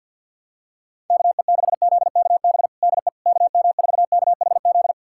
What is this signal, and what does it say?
Morse code, heard as a keyed tone, about 52 wpm, 700 Hz. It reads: KE6ZKB DE KM4KHZ
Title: USA Callsign DE Callsign 50wpm